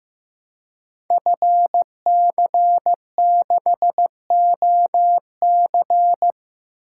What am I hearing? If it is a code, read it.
Morse code, heard as a keyed tone, about 15 wpm, 700 Hz. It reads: FC6OC
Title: Code Groups - 5 Letter:Numbers - 15wpm